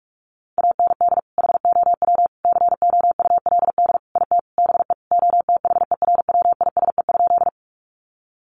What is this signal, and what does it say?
Morse code, heard as a keyed tone, about 44 wpm, 700 Hz. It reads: AND HOW COULD IT BE OTHERWISE?
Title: Sentences from Top 1200 Words 45wpm